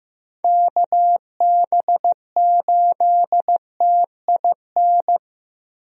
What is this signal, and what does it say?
Morse code, heard as a keyed tone, about 15 wpm, 700 Hz. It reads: KB8TIN
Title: USA Call Signs - 6 Characters - 15wpm